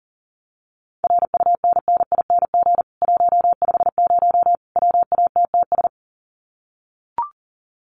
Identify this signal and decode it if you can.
Morse code, heard as a keyed tone, about 40 wpm, 700 Hz. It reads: RUNNING 150 WATTS
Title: Common QSO Phrases 40wpm